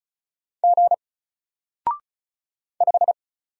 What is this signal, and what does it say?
Morse code, heard as a keyed tone, about 36 wpm, 700 Hz. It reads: G 5
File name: Single-Letter-Number-35wpm